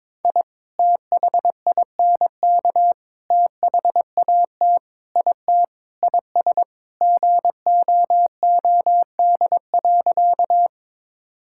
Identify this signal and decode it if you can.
Morse code, heard as a keyed tone, about 22 wpm, 700 Hz. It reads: I THINK THAT IT IS GOOD.
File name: practice / 22wpm / Sentences with 3-Word Phrases - 22wpm